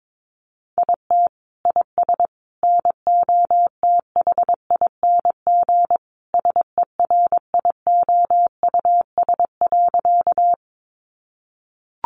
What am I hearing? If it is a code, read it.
Morse code, heard as a keyed tone, about 22 wpm, 700 Hz. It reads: IT IS NOTHING SERIOUS.
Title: Sentences from Top 1100 Words 22wpm